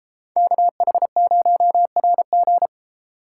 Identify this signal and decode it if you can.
Morse code, heard as a keyed tone, about 32 wpm, 700 Hz. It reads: KH0RG